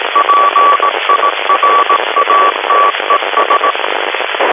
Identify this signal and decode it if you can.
Morse code, heard as a keyed tone, about 18 wpm, 1140 Hz, with radio noise. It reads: PIRATES
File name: 28.275 mhz nfm